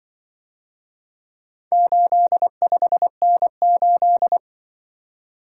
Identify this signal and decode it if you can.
Morse code, heard as a keyed tone, about 25 wpm, 700 Hz. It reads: 85N8
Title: Speed-Racing - Four Letter-Number Combinations - 20wpm